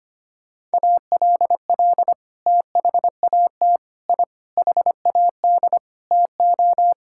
Morse code, heard as a keyed tone, about 25 wpm, 700 Hz. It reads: ALL THAT I HAD TO
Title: Sentences from Top 1000 Words - Review Part 6 - 25wpm